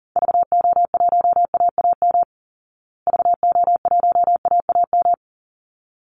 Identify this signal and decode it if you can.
Morse code, heard as a keyed tone, about 40 wpm, 700 Hz. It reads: VO1AAM VO1AAM
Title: Call Signs Repeated 3x - 40wpm